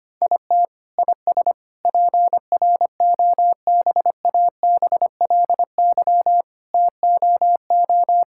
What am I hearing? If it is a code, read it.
Morse code, heard as a keyed tone, about 25 wpm, 700 Hz. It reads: IT IS PROBABLY TOO
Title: Sentences from Top 1000 Words - Review Part 6 - 25wpm